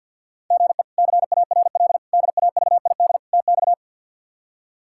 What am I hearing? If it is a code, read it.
Morse code, heard as a keyed tone, about 52 wpm, 700 Hz. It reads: GE CARL DAVID TX
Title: K1USN SST QSO Practice - 50wpm